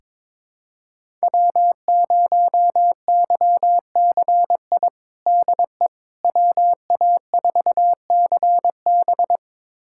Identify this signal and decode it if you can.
Morse code, heard as a keyed tone, about 21 wpm, 700 Hz. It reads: W0YCI DE WA4CB
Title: USA Callsign DE Callsign - 5 Characters - 22wpm